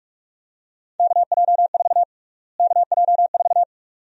Morse code, heard as a keyed tone, about 42 wpm, 700 Hz. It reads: KJ4 KJ4
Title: USA Call Sign Prefixes - Repeated 2x - 45wpm